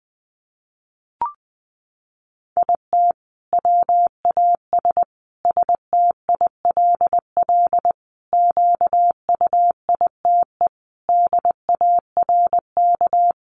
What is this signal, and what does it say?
Morse code, heard as a keyed tone, about 20 wpm, 700 Hz. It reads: IT WAS STILL QUITE DARK
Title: Sentences from Top 900 Words 20wpm